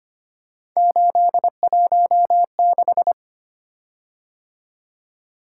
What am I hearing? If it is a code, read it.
Morse code, heard as a keyed tone, about 25 wpm, 700 Hz. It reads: 816
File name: Three Letter-Number Combinations - 25wpm